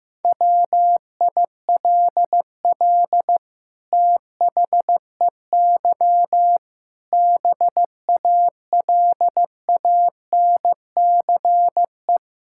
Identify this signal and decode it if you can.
Morse code, heard as a keyed tone, about 14 wpm, 700 Hz. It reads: WILL THEY BALANCE
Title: Sentences from Top 1000 Words - Review Part 6 - 15wpm